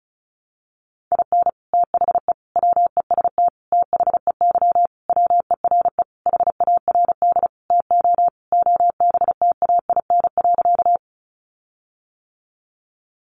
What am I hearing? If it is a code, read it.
Morse code, heard as a keyed tone, about 34 wpm, 700 Hz. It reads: IN THE WEST THEY WERE HARD TO OBTAIN.
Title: Sentences from Top 1300 Words 35wpm